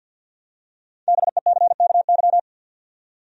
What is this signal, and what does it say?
Morse code, heard as a keyed tone, about 50 wpm, 700 Hz. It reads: BECKY
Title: USA Common Names - 50wpm